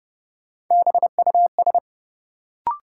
Morse code, heard as a keyed tone, about 30 wpm, 700 Hz. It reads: BUS
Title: Sentences from Top 1000 Words - Review Part 6 - New Word Drill - 30wpm